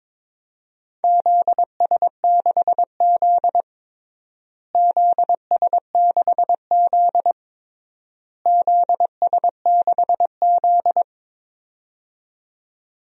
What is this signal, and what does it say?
Morse code, heard as a keyed tone, about 22 wpm, 700 Hz. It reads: ZS6Z ZS6Z ZS6Z
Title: DXCC-100 Call Signs - 4 Characters - Repeated 3x - 22wpm